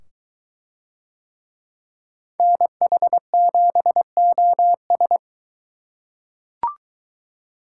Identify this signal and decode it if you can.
Morse code, heard as a keyed tone, about 22 wpm, 700 Hz. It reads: NH7OS